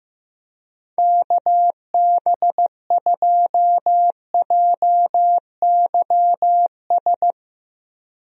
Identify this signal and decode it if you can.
Morse code, heard as a keyed tone, about 15 wpm, 700 Hz. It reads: KB2JYS
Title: USA Call Signs - 6 Characters - 15wpm